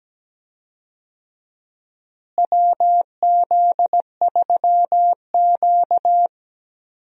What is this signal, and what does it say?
Morse code, heard as a keyed tone, about 17 wpm, 700 Hz. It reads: WZ3Q
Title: USA Call Signs - 4 Characters - Repeated 2x - 17wpm